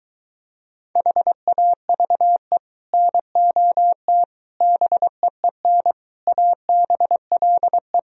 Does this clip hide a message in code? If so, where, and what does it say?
Yes: Morse code from 1.0 to 8.0 s, heard as a keyed tone, about 23 wpm, 700 Hz. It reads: HAVE NOT BEEN ABLE